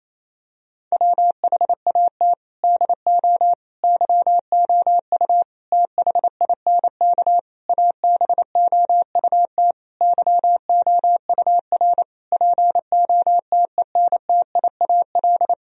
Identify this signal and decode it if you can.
Morse code, heard as a keyed tone, about 28 wpm, 700 Hz. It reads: WHAT DO YOU THINK ABOUT YOUR POTENTIAL
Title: Sentences from Top 1000 Words - Review Part 4 - 28wpm